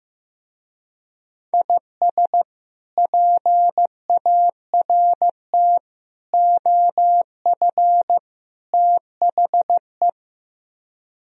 Morse code, heard as a keyed tone, about 15 wpm, 700 Hz. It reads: IS PART OF THE
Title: Speed-Racing - Sets of 4 Words 15wpm